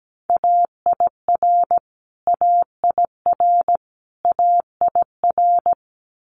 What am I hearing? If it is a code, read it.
Morse code, heard as a keyed tone, about 17 wpm, 700 Hz. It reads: AIR AIR AIR
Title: Top 200 Words Repeated 17wpm